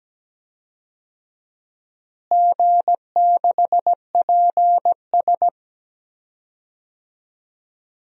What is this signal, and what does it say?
Morse code, heard as a keyed tone, about 17 wpm, 700 Hz. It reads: G6PS
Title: DXCC-100 Call Signs - 4 Characters - Repeated 2x - 17wpm